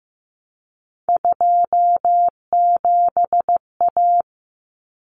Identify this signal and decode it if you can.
Morse code, heard as a keyed tone, about 15 wpm, 700 Hz. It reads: 27A
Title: Common Cut Numbers - 3-Digit 15wpm